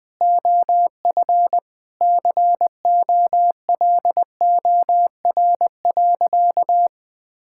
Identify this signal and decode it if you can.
Morse code, heard as a keyed tone, about 20 wpm, 700 Hz. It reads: OF COLOR.
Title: Sentences with 6-Word Phrases - 20wpm